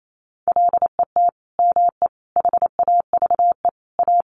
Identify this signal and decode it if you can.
Morse code, heard as a keyed tone, about 28 wpm, 700 Hz. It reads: LET ME HAVE A
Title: Sentences from Top 300 Words 28wpm